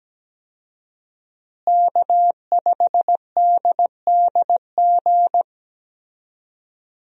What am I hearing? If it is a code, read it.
Morse code, heard as a keyed tone, about 17 wpm, 700 Hz. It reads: K5DDG
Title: USA Call Signs - 5 Characters - Repeated 2x - 17wpm